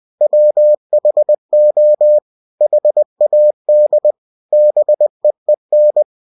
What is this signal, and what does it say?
Morse code, heard as a keyed tone, about 20 wpm, 600 Hz. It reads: WHO HAD BEEN
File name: cw-books / winnie-the-pooh / winnie-20wpm-page-0064